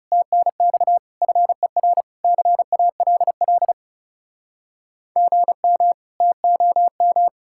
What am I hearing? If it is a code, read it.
Morse code, heard as a keyed tone, about 36 wpm, 700 Hz. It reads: TNX FER CALL GM TOM
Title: Speed-Racing Common QSO Phrases - 30wpm